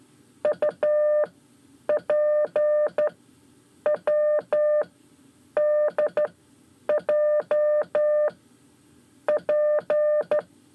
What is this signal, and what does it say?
Morse code, heard as a keyed tone, about 13 wpm, 600 Hz. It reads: U P W D J P
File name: 5 Lezione CW 1